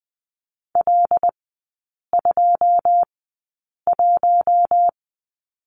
Morse code, heard as a keyed tone, about 20 wpm, spaced as for 8 wpm, 690 Hz. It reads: L21
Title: 145 Lesson 24e - Letter y - Three Characters 20wpm